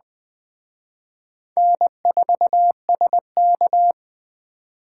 Morse code, heard as a keyed tone, about 20 wpm, 700 Hz. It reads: N4SK
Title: Speed Racing - USA 1x2 Call Signs 17wpm